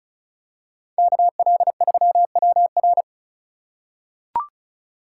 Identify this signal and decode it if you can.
Morse code, heard as a keyed tone, about 34 wpm, 700 Hz. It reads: KL3WR
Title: USA Call Signs - 5 Characters - 4x Character Spacing - 35wpm